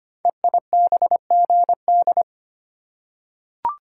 Morse code, heard as a keyed tone, about 25 wpm, 700 Hz. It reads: EIBGD
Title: Code Groups - 5 Letters - 25wpm